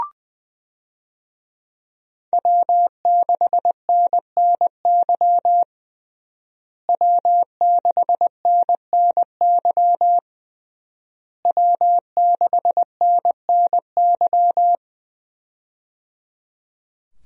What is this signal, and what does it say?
Morse code, heard as a keyed tone, about 20 wpm, 700 Hz. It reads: W6NNY W6NNY W6NNY
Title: USA Call Signs - 5 Characters - Repeated 3x - 20wpm